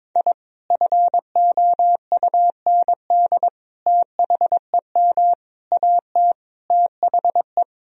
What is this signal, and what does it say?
Morse code, heard as a keyed tone, about 22 wpm, 700 Hz. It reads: I FOUND THEM AT THE
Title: Sentences from Top 1000 Words - Review Part 2 - 22wpm